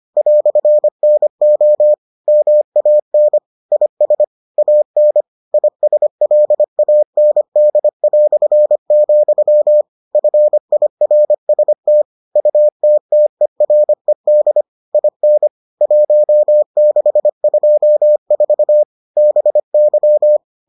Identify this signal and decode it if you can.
Morse code, heard as a keyed tone, about 25 wpm, 600 Hz. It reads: "NO MAN IS AN ISLAND", FIRST UTTERED IN 1624 BY